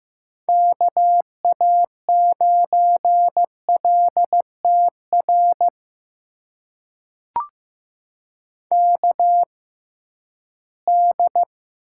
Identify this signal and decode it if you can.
Morse code, heard as a keyed tone, about 15 wpm, spaced as for 11 wpm, 700 Hz. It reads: KA9LTR KD
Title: USA Call Signs - 6 Characters - 6x Character Spacing - 15wpm